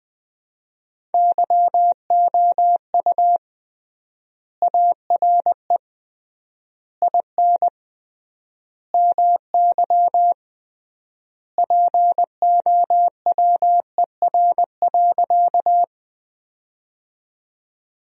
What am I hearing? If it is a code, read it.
Morse code, heard as a keyed tone, about 20 wpm, 700 Hz. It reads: YOU ARE IN MY POWER.
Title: 3x Word-Spacing - Sentences from Top 500 Words - 20wpm